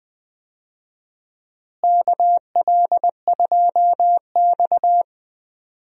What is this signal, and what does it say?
Morse code, heard as a keyed tone, about 20 wpm, 700 Hz. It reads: KL2X
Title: USA Call Signs - 4 Characters - Repeated 3x - 20wpm